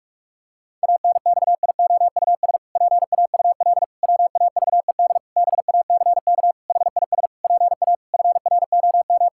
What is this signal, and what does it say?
Morse code, heard as a keyed tone, about 41 wpm, 700 Hz. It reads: ANXIOUS PAUL WAVED BACK HIS PA FROM
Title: Pangram Challenge v2 - 45wpm